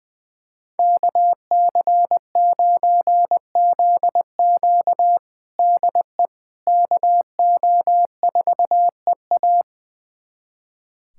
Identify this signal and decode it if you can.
Morse code, heard as a keyed tone, about 20 wpm, 700 Hz. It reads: KC9ZQ DE KO4EA
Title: USA Callsign DE Callsign - 5 Characters - 20wpm